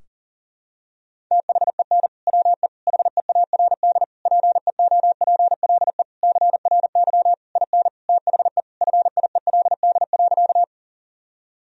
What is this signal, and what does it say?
Morse code, heard as a keyed tone, about 40 wpm, 700 Hz. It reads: THEN WE HEARD PEOPLE CRY IN THE FIELD.
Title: Sentences from Top 1000 Words - Review Part 2 - 40wpm